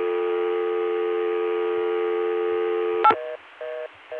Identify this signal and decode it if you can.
Dial tone (North America), 350 + 440 Hz.